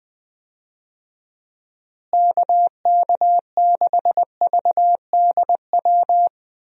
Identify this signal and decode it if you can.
Morse code, heard as a keyed tone, about 20 wpm, 700 Hz. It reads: KK6VDW
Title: USA Call Signs - 6 Characters - Repeated 3x - 20wpm